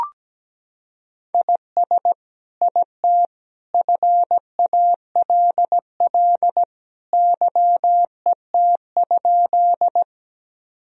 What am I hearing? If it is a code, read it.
Morse code, heard as a keyed tone, about 17 wpm, 700 Hz. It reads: IS IT FALL YET?
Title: Sentences from Top 1000 Words - Review Part 2 - 17wpm